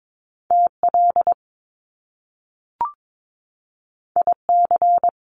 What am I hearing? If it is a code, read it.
Morse code, heard as a keyed tone, about 22 wpm, 700 Hz. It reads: TL IC
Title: Top Two Letter Combinations in Words 22wpm